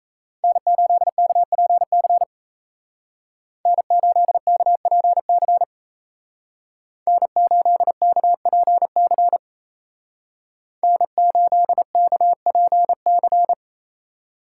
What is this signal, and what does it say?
Morse code, heard as a keyed tone, about 36 wpm, 700 Hz. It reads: N8KPC N8KPC N8KPC N8KPC